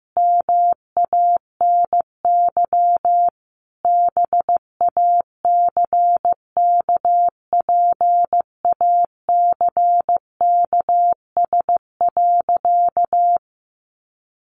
Morse code, heard as a keyed tone, about 15 wpm, 700 Hz. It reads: MANY BACKPACKS.
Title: Additional Easy Phrases 15wpm